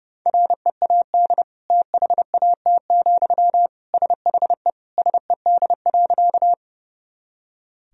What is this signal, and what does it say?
Morse code, heard as a keyed tone, about 30 wpm, 700 Hz. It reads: READ THAT, SHE SED.
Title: Sentences from Top 200 Words and Abbreviations 30wpm